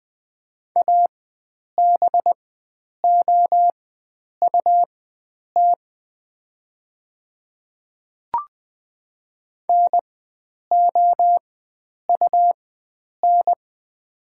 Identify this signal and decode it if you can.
Morse code, heard as a keyed tone, about 20 wpm, spaced as for 9 wpm, 690 Hz. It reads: ABOUT NOUN
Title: 4x Character Spacing - Top 500 Words - 20wpm